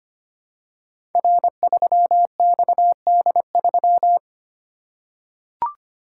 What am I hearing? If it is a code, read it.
Morse code, heard as a keyed tone, about 25 wpm, 700 Hz. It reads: R3XD3
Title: Code Groups - 5 Letter:Numbers - 25wpm